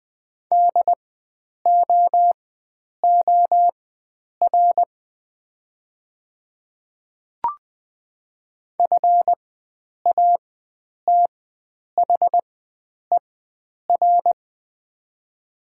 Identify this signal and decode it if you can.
Morse code, heard as a keyed tone, about 20 wpm, spaced as for 9 wpm, 700 Hz. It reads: DOOR FATHER
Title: 4x Character Spacing - Top 500 Words - 20wpm